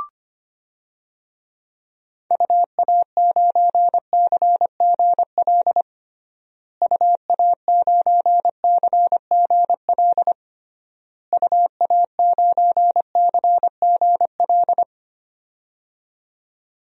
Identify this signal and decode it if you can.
Morse code, heard as a keyed tone, about 25 wpm, 700 Hz. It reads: UA9CGL UA9CGL UA9CGL
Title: DXCC-100 Call Signs - 6 Characters - Repeated 3x - 25wpm